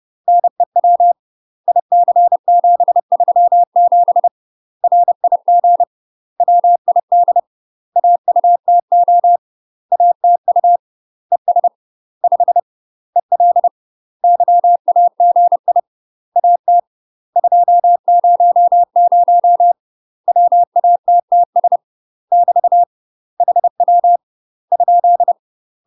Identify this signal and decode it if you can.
Morse code, heard as a keyed tone, about 30 wpm, 700 Hz. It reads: NEW IC737 RIG WID AUTO ATU ES 5 EL YAGI AT 200 WATTS <BT> HW ?
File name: UK MASTER QSOs TRAINING / 30WPM / UK MASTER QSOs 0049